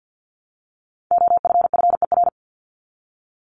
Morse code, heard as a keyed tone, about 50 wpm, 700 Hz. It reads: OFFER
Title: ED Suffix - Top 100 Words with Base Words 50wpm